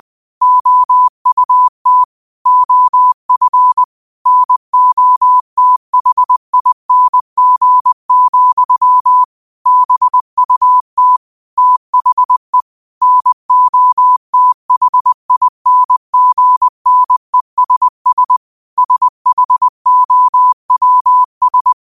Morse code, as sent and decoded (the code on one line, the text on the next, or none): --- ..- - / --- ..-. / -. --- - .... .. -. --. --..-- / -... ..- - / - .... / -. --- - .... .. -. --. -. ... ... / ... .... --- .-- ...
OUT OF NOTHING, BUT THE NOTHINGNESS SHOWS